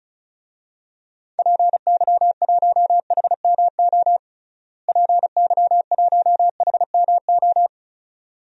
Morse code, heard as a keyed tone, about 36 wpm, 700 Hz. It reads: PY1HMO PY1HMO
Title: DXCC-100 Call Signs - 6 Characters - Repeated 2x - 35wpm